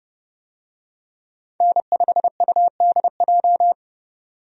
Morse code, heard as a keyed tone, about 32 wpm, 700 Hz. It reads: N5UDJ
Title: USA Call Signs - 5 Characters - 30wpm